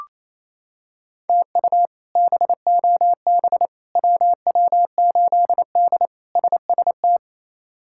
Morse code, heard as a keyed tone, about 28 wpm, 700 Hz. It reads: TU BOB WW8D SST
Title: K1USN SST QSO Practice - 28wpm